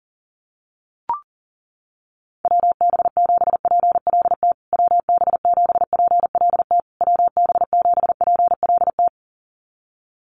Morse code, heard as a keyed tone, about 40 wpm, 700 Hz. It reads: WB7PLT WB7PLT WB7PLT
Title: USA Call Signs Repeated 3x 40wpm